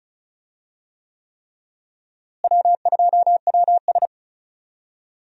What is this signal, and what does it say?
Morse code, heard as a keyed tone, about 36 wpm, 700 Hz. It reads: W2WS
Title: USA Call Signs - 4 Characters - Repeated 3x - 35wpm